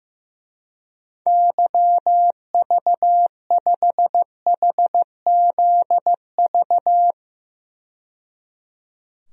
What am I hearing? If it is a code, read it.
Morse code, heard as a keyed tone, about 15 wpm, 700 Hz. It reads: YV5HZV
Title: DXCC-100 Call Signs - 6 Characters - 15wpm